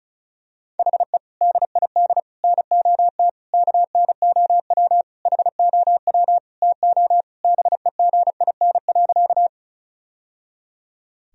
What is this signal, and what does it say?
Morse code, heard as a keyed tone, about 36 wpm, 700 Hz. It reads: HE DID NOT KNOW HOW TO BEGIN.
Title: Sentences from Top 1000 Words - Review Part 1 - 35wpm